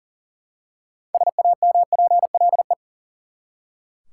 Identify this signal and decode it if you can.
Morse code, heard as a keyed tone, about 40 wpm, 700 Hz. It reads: SAMPLE
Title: Six Letter Words - 40wpm